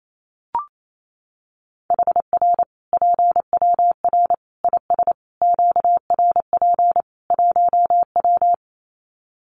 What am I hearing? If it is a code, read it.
Morse code, heard as a keyed tone, about 28 wpm, 700 Hz. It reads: HR PWR IS QRP 1W
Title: Common QSO Phrases 28wpm